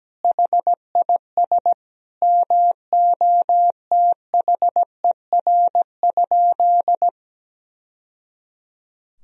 Morse code, heard as a keyed tone, about 17 wpm, 700 Hz. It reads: HIS MOTHER?
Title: Sentences from Top 1000 Words - Review Part 2 - 17wpm